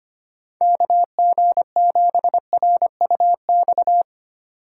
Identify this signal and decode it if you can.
Morse code, heard as a keyed tone, about 25 wpm, 700 Hz. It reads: KG7RUX
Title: USA Call Signs - 2x Character Spacing - 25wpm